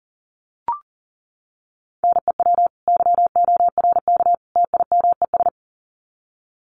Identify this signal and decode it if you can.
Morse code, heard as a keyed tone, about 40 wpm, 700 Hz. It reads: NEW YORK TIMES
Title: Sets of 3 Words 40wpm